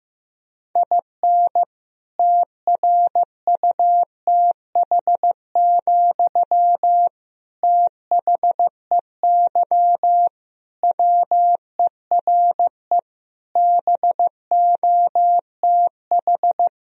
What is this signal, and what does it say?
Morse code, heard as a keyed tone, about 15 wpm, 700 Hz. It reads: IN TRUTH, THEY WERE BOTH
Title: Sentences from Top 1000 Words - Review Part 1 - 15wpm